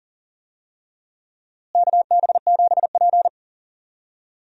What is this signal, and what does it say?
Morse code, heard as a keyed tone, about 40 wpm, 700 Hz. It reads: KB7P
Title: USA Call Signs - 4 Characters - Repeated 3x - 40wpm